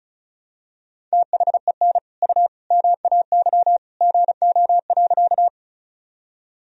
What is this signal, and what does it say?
Morse code, heard as a keyed tone, about 34 wpm, 700 Hz. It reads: THEN U MAY GO.
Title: Sentences from Top 100 Words and Abbreviations 35wpm